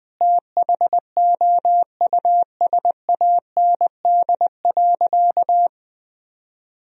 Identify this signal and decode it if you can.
Morse code, heard as a keyed tone, about 20 wpm, 700 Hz. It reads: THOUSAND.
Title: Sentences from Top 1000 Words - Review Part 3 - 20wpm